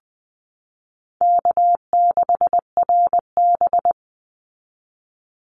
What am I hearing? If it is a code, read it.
Morse code, heard as a keyed tone, about 20 wpm, 700 Hz. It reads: K6RB
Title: Easy Call Signs Repeated 2x - 20wpm